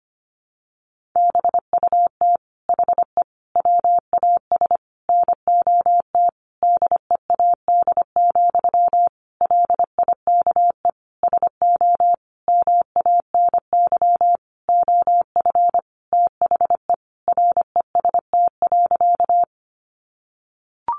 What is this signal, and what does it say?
Morse code, heard as a keyed tone, about 25 wpm, 700 Hz. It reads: BUT HE WAS NOT DEAD, LIKE SO MANY OF THE REST.
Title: Sentences from Top 1500 Words 25wpm